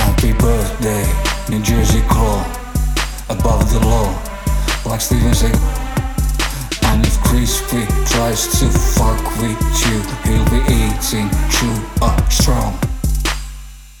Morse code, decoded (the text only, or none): none